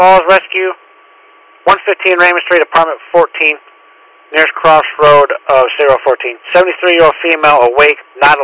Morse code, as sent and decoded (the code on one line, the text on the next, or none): none